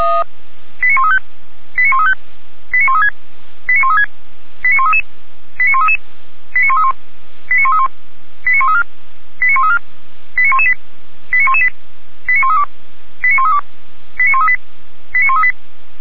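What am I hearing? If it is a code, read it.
Five-tone selective call (ZVEI), heard as quick runs of five tones, tones of about 70 ms. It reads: 89135 89135 89136 89136 89133 89133 89131 89131 89134 89134 89118 89118 89132 89132 89138 89138
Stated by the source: ZVEI five-tone received by SDR, Feuerwehr Orlamünde.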